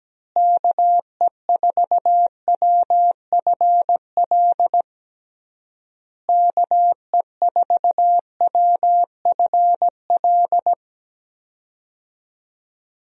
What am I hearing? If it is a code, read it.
Morse code, heard as a keyed tone, about 17 wpm, 700 Hz. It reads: KE4WFL KE4WFL
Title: USA Call Signs - 6 Characters - Repeated 3x - 17wpm